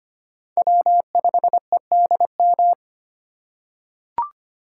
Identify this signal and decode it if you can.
Morse code, heard as a keyed tone, about 25 wpm, 700 Hz. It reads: W5EDM
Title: USA Call Signs - 5 Characters - 4x Character Spacing - 25wpm